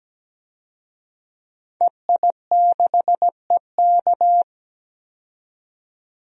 Morse code, heard as a keyed tone, about 17 wpm, 700 Hz. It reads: EI6EK
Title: DXCC-100 Call Signs - 5 Characters - 17wpm